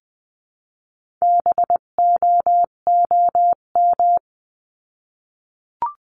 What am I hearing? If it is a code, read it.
Morse code, heard as a keyed tone, about 20 wpm, 700 Hz. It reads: BOOM
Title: 258 Lesson 42c - Words 16-20wpm